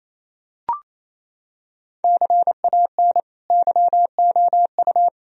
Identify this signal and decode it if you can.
Morse code, heard as a keyed tone, about 28 wpm, 700 Hz. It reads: CAN YOU
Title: Sentences from Top 1000 Words - Review Part 3 - 28wpm